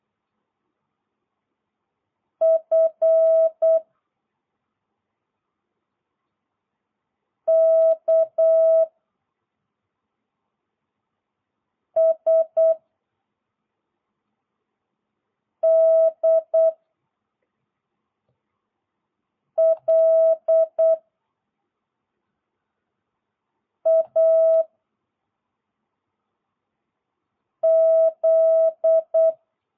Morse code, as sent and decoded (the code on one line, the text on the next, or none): ..-. / -.- / ... / -.. / .-.. / .- / --..
F K S D L A Z